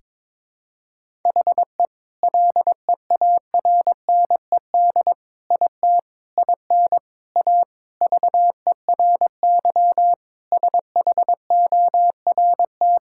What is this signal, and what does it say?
Morse code, heard as a keyed tone, about 22 wpm, 700 Hz. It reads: HE LEARNED IT IN A VERY SHORT
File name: Sentences with 7-Word Phrases - 22wpm